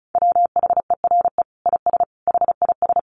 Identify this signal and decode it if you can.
Morse code, heard as a keyed tone, about 34 wpm, 700 Hz. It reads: WHERE IS HIS
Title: Sentences from Top 1200 Words 35wpm